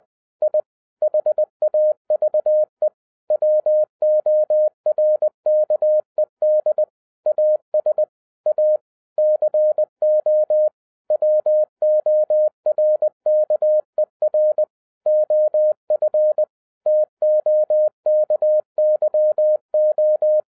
Morse code, heard as a keyed tone, about 20 wpm, 610 Hz. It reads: I HAVE WORKED AS A CO WORKER OF TOKYO
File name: 2011-10-mp3 / n03-TARO JR0QWW 20wpm